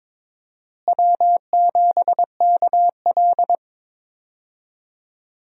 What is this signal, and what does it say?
Morse code, heard as a keyed tone, about 22 wpm, 700 Hz. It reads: W7KL
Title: USA Call Signs - 4 Characters - Repeated 2x - 22wpm